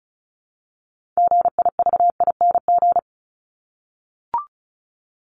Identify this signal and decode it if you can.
Morse code, heard as a keyed tone, about 34 wpm, 700 Hz. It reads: GIVING
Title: ING Suffix - Top 100 Words with Base Words 35wpm